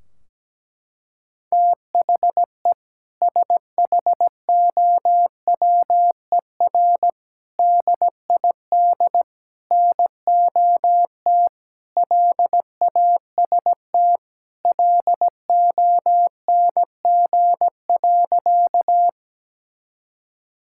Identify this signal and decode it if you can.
Morse code, heard as a keyed tone, about 17 wpm, 700 Hz. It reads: THE SHOWER DID NOT LAST LONG.
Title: Sentences from Top 1600 Words 17wpm